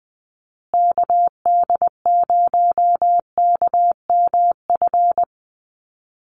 Morse code, heard as a keyed tone, about 20 wpm, 700 Hz. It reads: KD0KMF
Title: USA Call Signs Repeated 2x 20wpm